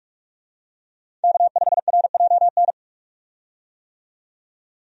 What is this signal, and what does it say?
Morse code, heard as a keyed tone, about 42 wpm, 700 Hz. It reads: K5RJN
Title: USA Call Signs - 5 Characters - 45wpm